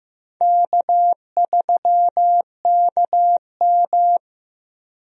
Morse code, heard as a keyed tone, about 16 wpm, 700 Hz. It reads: K3KM
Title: USA Call Signs - 4 Characters - Repeated 3x - 15wpm